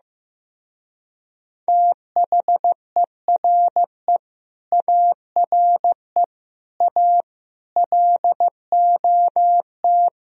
Morse code, heard as a keyed tone, about 15 wpm, 700 Hz. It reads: THERE ARE A LOT
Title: Speed-Racing - Sets of 5 Words 15wpm